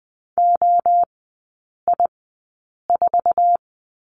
Morse code, heard as a keyed tone, about 20 wpm, spaced as for 8 wpm, 700 Hz. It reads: OI4
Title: 053 Lesson 10e - Letter r - Three Characters 20wpm